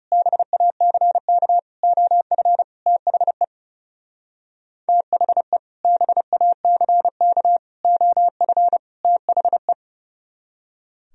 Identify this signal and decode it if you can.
Morse code, heard as a keyed tone, about 28 wpm, 700 Hz. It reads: BACK OF THE THE BACK OF THE
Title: Speed-Racing - Sets of 4 Words 30wpm